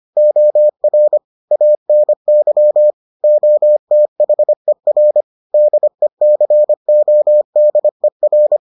Morse code, heard as a keyed tone, about 25 wpm, 600 Hz. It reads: OR ANY OTHER DECODER